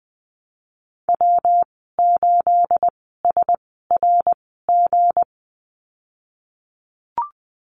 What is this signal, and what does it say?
Morse code, heard as a keyed tone, about 20 wpm, spaced as for 15 wpm, 700 Hz. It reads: W8SRG
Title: 213 Lesson 34g - Number 8 - Call Signs 20wpm